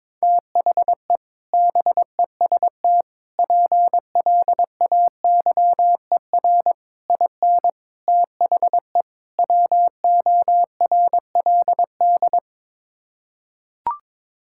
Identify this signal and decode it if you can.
Morse code, heard as a keyed tone, about 21 wpm, 700 Hz. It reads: THE BEST PLAYER IN THE WORLD
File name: practice / 22wpm / Sets of 6 Words - 22wpm